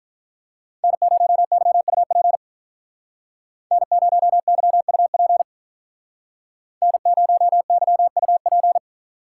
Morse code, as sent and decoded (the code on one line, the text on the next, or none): -. ----- -.-- ..- .--. / -. ----- -.-- ..- .--. / -. ----- -.-- ..- .--.
N0YUP N0YUP N0YUP